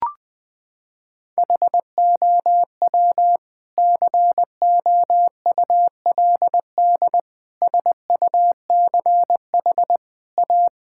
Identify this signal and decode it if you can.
Morse code, heard as a keyed tone, about 20 wpm, 700 Hz. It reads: HOW COULD SUCH A
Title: Sentences from Top 1000 Words - Review Part 6 - 20wpm